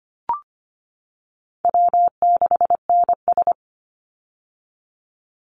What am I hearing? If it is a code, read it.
Morse code, heard as a keyed tone, about 25 wpm, 700 Hz. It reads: W6NS
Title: Easy Call Signs - 25wpm